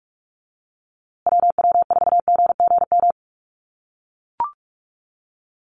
Morse code, heard as a keyed tone, about 44 wpm, 700 Hz. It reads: WW4GGM
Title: USA Call Signs Repeated 2x 45wpm